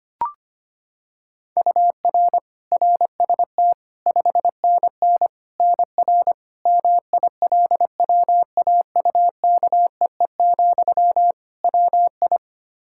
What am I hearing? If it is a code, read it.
Morse code, heard as a keyed tone, about 25 wpm, 700 Hz. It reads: UR RST 5NN NR MILWAUKEE, WI
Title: Practice QSO - Exchange 1 - Beginning - 25wpm